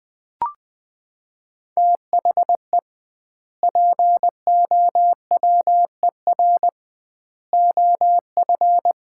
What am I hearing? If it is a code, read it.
Morse code, heard as a keyed tone, about 21 wpm, 700 Hz. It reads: THE POWER OF
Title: 2x Word-Spacing - Sets of 4 Words - 20wpm